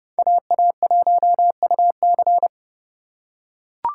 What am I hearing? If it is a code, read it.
Morse code, heard as a keyed tone, about 32 wpm, 700 Hz. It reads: AA1UC
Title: K1USN SST QSO Practice - 30wpm